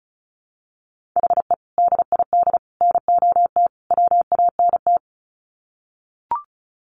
Morse code, heard as a keyed tone, about 34 wpm, 700 Hz. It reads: HE DID NOT WANT
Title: Sets of 4 Words 35wpm